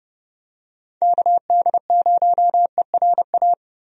Morse code, heard as a keyed tone, about 31 wpm, 700 Hz. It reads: KD0ERA
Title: USA Call Signs - 6 Characters - Repeated 2x - 30wpm